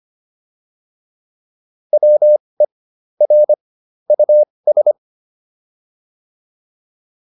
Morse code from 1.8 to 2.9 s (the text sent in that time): WE